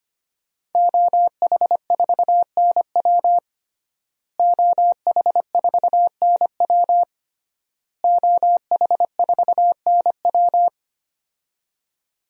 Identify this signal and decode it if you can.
Morse code, heard as a keyed tone, about 25 wpm, 700 Hz. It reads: OH4NW OH4NW OH4NW
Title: DXCC-100 Call Signs - 5 Characters - Repeated 3x - 25wpm